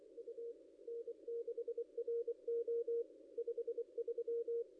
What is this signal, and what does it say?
Morse code, heard as a keyed tone, about 26 wpm, 470 Hz. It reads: U N6RO 53
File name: contests / ss-cw-k5tr-op-n5rz -0900